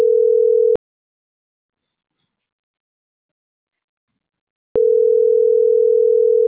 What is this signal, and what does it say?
Ringback tone (North America), 440 + 480 Hz.